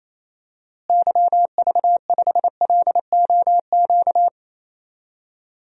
Morse code, heard as a keyed tone, about 28 wpm, 700 Hz. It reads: YV5LOQ
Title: DXCC-100 Call Signs - 6 Characters - Repeated 2x - 28wpm